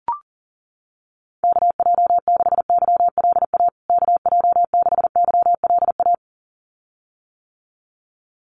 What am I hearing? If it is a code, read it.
Morse code, heard as a keyed tone, about 40 wpm, 700 Hz. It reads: KJ6YLA KJ6YLA
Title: USA Call Signs Repeated 2x 40wpm